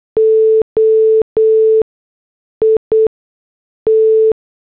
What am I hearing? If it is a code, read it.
Morse code, heard as a keyed tone, about 8 wpm, spaced as for 6 wpm, 440 Hz. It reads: O IT
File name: morse / headlines